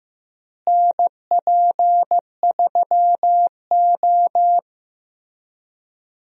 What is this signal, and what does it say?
Morse code, heard as a keyed tone, about 15 wpm, 700 Hz. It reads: NP3O
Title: USA Call Signs - 4 Characters - Repeated 2x - 15wpm